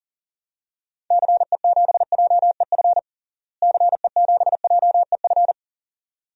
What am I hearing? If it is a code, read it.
Morse code, heard as a keyed tone, about 40 wpm, 700 Hz. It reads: CE7JEF CE7JEF
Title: DXCC-100 Call Signs - 6 Characters - Repeated 2x - 40wpm